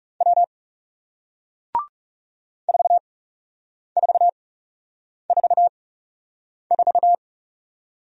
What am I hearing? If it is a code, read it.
Morse code, heard as a keyed tone, about 41 wpm, spaced as for 8 wpm, 700 Hz. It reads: W 4444